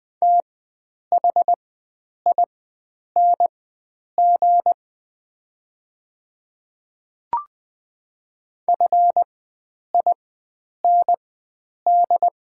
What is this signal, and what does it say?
Morse code, heard as a keyed tone, about 20 wpm, spaced as for 9 wpm, 700 Hz. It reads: THING FIND
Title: 4x Character Spacing - Top 100 Words - 20wpm